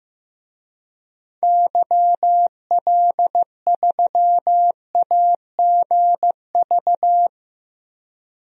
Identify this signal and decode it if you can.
Morse code, heard as a keyed tone, about 16 wpm, 700 Hz. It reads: YL3AGV
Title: DXCC-100 Call Signs - 6 Characters - Repeated 2x - 15wpm